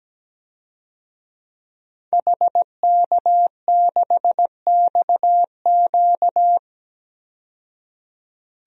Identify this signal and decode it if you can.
Morse code, heard as a keyed tone, about 17 wpm, 700 Hz. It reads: HK6XQ
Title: DXCC-100 Call Signs - 5 Characters - 17wpm